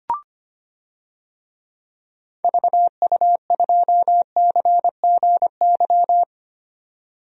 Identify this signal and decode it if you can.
Morse code, heard as a keyed tone, about 25 wpm, 700 Hz. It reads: VU2CGY
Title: DXCC-100 Call Signs - 6 Characters - 25wpm